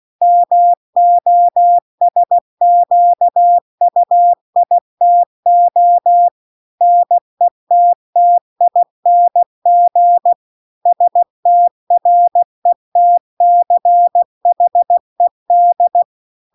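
Morse code, heard as a keyed tone, about 16 wpm, 700 Hz. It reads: MOSQUITO NETTING STRETCHED
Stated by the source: Morse practice text at 16 WPM.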